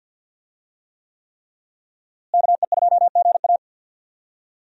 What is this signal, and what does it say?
Morse code, heard as a keyed tone, about 52 wpm, 700 Hz. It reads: KE2GA
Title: USA Call Signs - 5 Characters - Repeated 3x - 50wpm